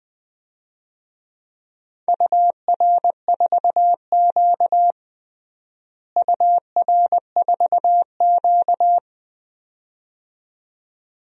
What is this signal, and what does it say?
Morse code, heard as a keyed tone, about 21 wpm, 700 Hz. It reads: UR4Q UR4Q
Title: DXCC-100 Call Signs - 4 Characters - Repeated 2x - 20wpm